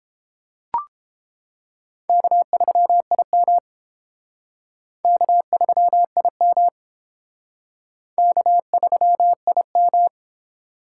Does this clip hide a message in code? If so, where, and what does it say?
Yes: Morse code from 2.1 to 10.1 s, heard as a keyed tone, about 30 wpm, 700 Hz. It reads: K3IM K3IM K3IM